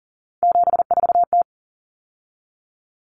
Morse code, heard as a keyed tone, about 40 wpm, 700 Hz. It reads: 74T
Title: Common Cut Numbers - 3-Digit 40wpm